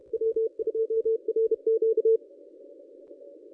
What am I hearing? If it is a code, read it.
Morse code, heard as a keyed tone, about 30 wpm, 450 Hz. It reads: W2RQ
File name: contests / ss-cw-k5tr-op-n5rz -1100